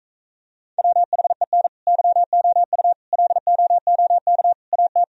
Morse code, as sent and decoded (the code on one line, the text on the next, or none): .-- .... -. / -.-- --- ..- / .-.. --- --- -.- / .- -
WHEN YOU LOOK AT